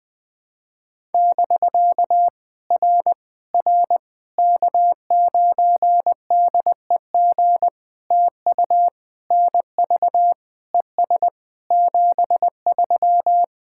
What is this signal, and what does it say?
Morse code, heard as a keyed tone, about 20 wpm, 690 Hz. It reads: <BK> R R K9DEG TU NV ES 73
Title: POTA QSO Practice - 20wpm